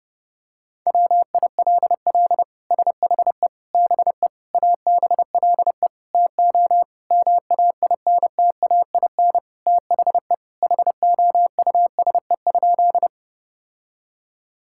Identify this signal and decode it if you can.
Morse code, heard as a keyed tone, about 32 wpm, 700 Hz. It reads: WILL SHE BE ABLE TO MAINTAIN THE HOUSE?
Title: Sentences from Top 1000 Words - Review Part 5 - 30wpm